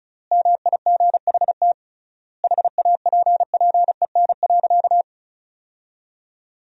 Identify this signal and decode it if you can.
Morse code, heard as a keyed tone, about 36 wpm, 700 Hz. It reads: MIGHT HAPPEN.
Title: 3x Word-Spacing - Sentences from Top 400 Words - 35wpm